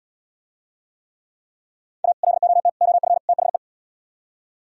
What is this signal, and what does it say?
Morse code, heard as a keyed tone, about 77 wpm, 700 Hz. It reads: A LOT OF THE